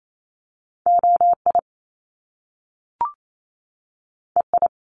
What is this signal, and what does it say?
Morse code, heard as a keyed tone, about 28 wpm, 700 Hz. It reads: OI EI
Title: Top Two Letter Combinations in Words 28wpm